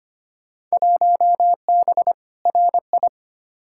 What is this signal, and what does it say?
Morse code, heard as a keyed tone, about 25 wpm, 700 Hz. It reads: 1B RI
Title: Field Day Exchanges Encore - 25wpm